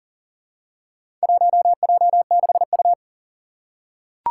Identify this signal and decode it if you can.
Morse code, heard as a keyed tone, about 40 wpm, 700 Hz. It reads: 1J6U
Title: Four Challenging Letter-Number Combinations 40wpm